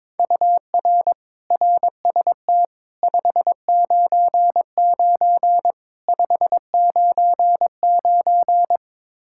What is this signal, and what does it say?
Morse code, heard as a keyed tone, about 22 wpm, 700 Hz. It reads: UR RST 599 599
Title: Practice QSO - Exchange 1 - Beginning - 22wpm